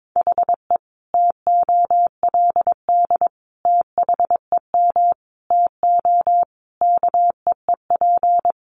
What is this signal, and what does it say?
Morse code, heard as a keyed tone, about 22 wpm, 700 Hz. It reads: HE TOLD THEM TO KEEP
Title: Sentences from Top 1200 Words 22wpm